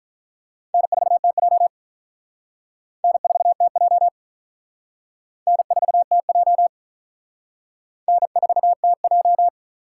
Morse code, heard as a keyed tone, about 42 wpm, 700 Hz. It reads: N4TJ N4TJ N4TJ N4TJ
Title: Speed Racing - USA Call Signs 35wpm